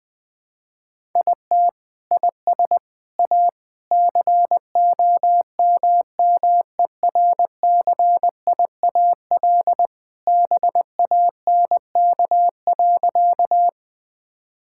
Morse code, heard as a keyed tone, about 20 wpm, 700 Hz. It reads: IT IS A COMMERCIAL BANK.
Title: Sentences from Top 1000 Words - Review Part 6 - 20wpm